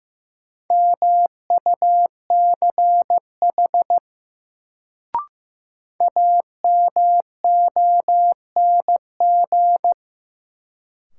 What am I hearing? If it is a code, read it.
Morse code, heard as a keyed tone, about 15 wpm, 700 Hz. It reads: MUCH AMONG
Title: Sets of 2 Words - New Word Drill - 15wpm